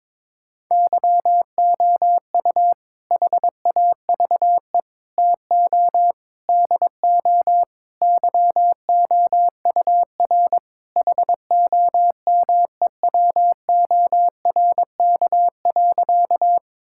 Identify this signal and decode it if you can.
Morse code, heard as a keyed tone, about 22 wpm, 700 Hz. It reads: YOU HAVE TO DO YOUR HOMEWORK.
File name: practice / 22wpm / Sentences with 4-Word Phrases - 22wpm